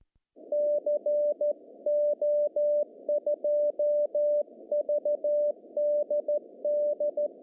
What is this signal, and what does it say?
Morse code, heard as a keyed tone, about 14 wpm, 590 Hz. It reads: CO2VDD
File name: CW / cqww-cw-k5tr